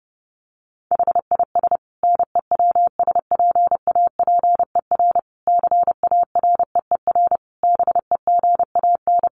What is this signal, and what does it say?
Morse code, heard as a keyed tone, about 30 wpm, 700 Hz. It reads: HIS NEWSPAPER CAREER BEGAN
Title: Sentences from Top 1100 Words 30wpm